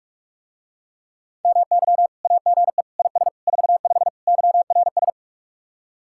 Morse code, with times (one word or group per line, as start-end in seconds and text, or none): 1.4-2.1 MY
2.2-2.8 AGE
3.0-3.3 IS
3.5-4.1 45
4.3-5.1 YRS